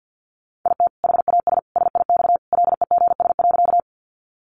Morse code, heard as a keyed tone, about 50 wpm, 700 Hz. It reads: IT HAS SIX LEGS.
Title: Additional Easy Phrases 50wpm